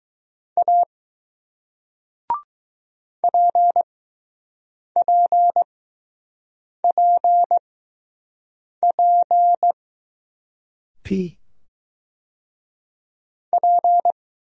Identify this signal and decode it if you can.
Morse code, heard as a keyed tone, about 20 wpm, spaced as for 6 wpm, 700 Hz. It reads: A PPPP P